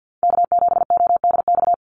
Morse code, heard as a keyed tone, about 50 wpm, 700 Hz. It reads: K7ODX
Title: NA SOTA Activator Callsigns 50wpm